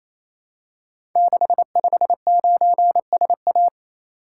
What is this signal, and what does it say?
Morse code, heard as a keyed tone, about 27 wpm, 700 Hz. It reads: 659SA
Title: Code Groups - 5 Letter:Numbers - 28wpm